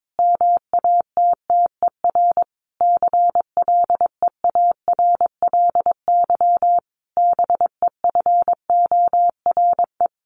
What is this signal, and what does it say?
Morse code, heard as a keyed tone, about 22 wpm, 700 Hz. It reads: MATTER CLEARLY BEFORE
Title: Sentences from Top 1300 Words 22wpm